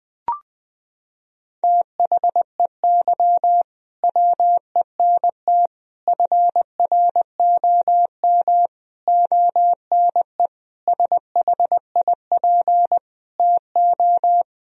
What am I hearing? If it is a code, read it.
Morse code, heard as a keyed tone, about 20 wpm, 700 Hz. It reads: THEY WENT FROM ONE SHIP TO
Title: Sentences from Top 1000 Words - Review Part 2 - 20wpm